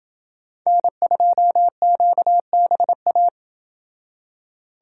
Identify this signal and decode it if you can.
Morse code, heard as a keyed tone, about 28 wpm, 700 Hz. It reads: N2QBA